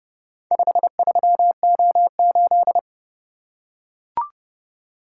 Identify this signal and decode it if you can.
Morse code, heard as a keyed tone, about 29 wpm, 700 Hz. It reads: 53O8